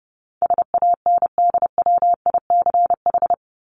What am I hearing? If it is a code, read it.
Morse code, heard as a keyed tone, about 30 wpm, 700 Hz. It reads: SANDWICH
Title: Top 1751-2000 Words 30wpm